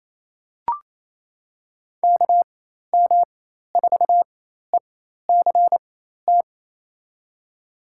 Morse code, heard as a keyed tone, about 27 wpm, spaced as for 13 wpm, 700 Hz. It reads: KM4ECT
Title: USA Call Signs - 6 Characters - 4x Character Spacing - 28wpm